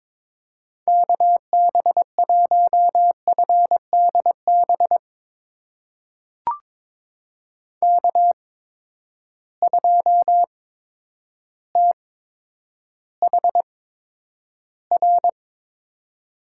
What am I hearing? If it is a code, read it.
Morse code, heard as a keyed tone, about 22 wpm, spaced as for 10 wpm, 700 Hz. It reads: KB1FDB K2THR
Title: USA Call Signs - 8x Character Spacing - 22wpm